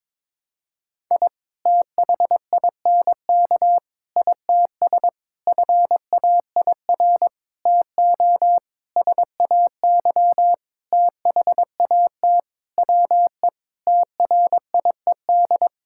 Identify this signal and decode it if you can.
Morse code, heard as a keyed tone, about 22 wpm, 700 Hz. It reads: I THINK ITS FAIR TO SAY THAT WE TRIED
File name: Sentences with 7-Word Phrases - 22wpm